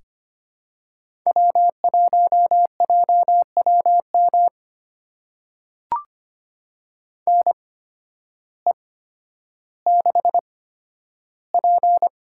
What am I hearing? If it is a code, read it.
Morse code, heard as a keyed tone, about 25 wpm, spaced as for 12 wpm, 700 Hz. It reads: W1JWM NE6P
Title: USA Call Signs - 8x Character Spacing - 25wpm